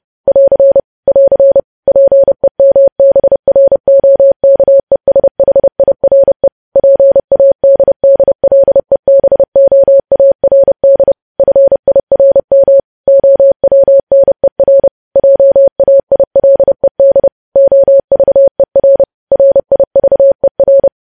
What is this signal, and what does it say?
Morse code, heard as a keyed tone, about 30 wpm, 550 Hz. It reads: <AR> <AR> PEMBROKESHIRE PADDLEBOARD FIRM OWNER JAILED OVER RIVER
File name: podcast / News headlines-1x-30x30